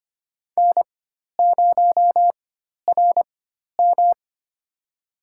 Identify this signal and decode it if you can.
Morse code, heard as a keyed tone, about 25 wpm, spaced as for 12 wpm, 700 Hz. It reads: N0RM
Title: USA Call Signs - 4x Character Spacing - 25wpm